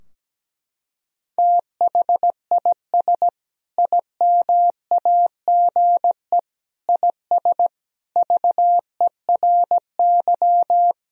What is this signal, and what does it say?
Morse code, heard as a keyed tone, about 17 wpm, 700 Hz. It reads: THIS IMAGE IS VERY
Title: Sentences from Top 1000 Words - Review Part 4 - 17wpm